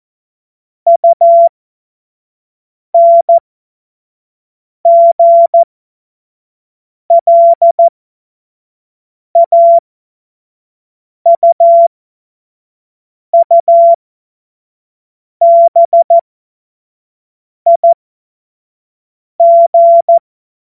Morse code, heard as a keyed tone, about 14 wpm, spaced as for 5 wpm, 680 Hz. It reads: UNGLAUUBIG
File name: Morsekurs / Witze / Witz 19 Dezember 2023 25 F70